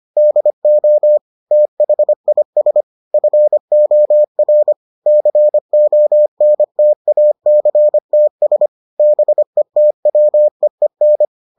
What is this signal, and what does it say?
Morse code, heard as a keyed tone, about 25 wpm, 600 Hz. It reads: DO THIS FOR CONTACTS BETWEEN